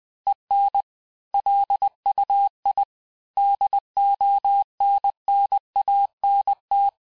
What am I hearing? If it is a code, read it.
Morse code, heard as a keyed tone, about 20 wpm, 800 Hz. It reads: EN LUI DONNANT
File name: cours-du-mercredi / audios / cours-858-texte-clais-vitesse-19mpm